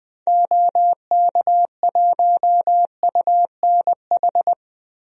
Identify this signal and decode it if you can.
Morse code, heard as a keyed tone, about 20 wpm, 700 Hz. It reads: OK1UNH
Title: DXCC-100 Call Signs - 6 Characters - 20wpm